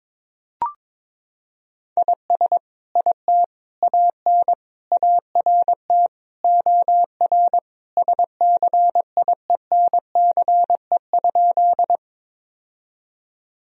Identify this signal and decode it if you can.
Morse code, heard as a keyed tone, about 22 wpm, 700 Hz. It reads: IS IT AN ART OR SCIENCE?
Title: Sentences from Top 1000 Words - Review Part 3 - 22wpm